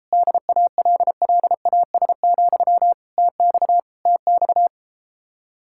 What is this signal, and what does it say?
Morse code, heard as a keyed tone, about 33 wpm, 700 Hz. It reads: DALLAS, TX TX